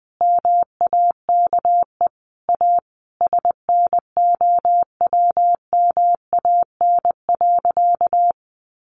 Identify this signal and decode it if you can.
Morse code, heard as a keyed tone, about 20 wpm, 700 Hz. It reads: MAKE A SNOWMAN.
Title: Easy Phrases 20wpm